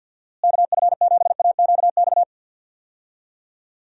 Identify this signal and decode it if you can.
Morse code, heard as a keyed tone, about 52 wpm, 700 Hz. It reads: KF7AQX
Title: USA Call Signs - 6 Characters - Repeated 3x - 50wpm